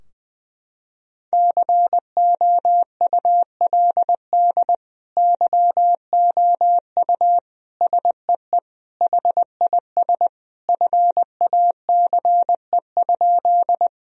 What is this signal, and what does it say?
Morse code, heard as a keyed tone, about 21 wpm, 700 Hz. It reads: COULD YOU SEE HIS FACE?
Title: Sentences from Top 1000 Words - Review Part 2 - 20wpm